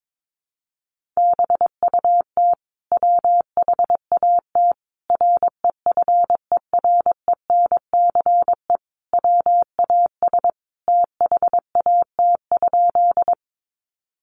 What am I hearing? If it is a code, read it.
Morse code, heard as a keyed tone, about 22 wpm, 700 Hz. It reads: BUT WHAT REFERENCE WAS THAT?
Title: Sentences from Top 1200 Words 22wpm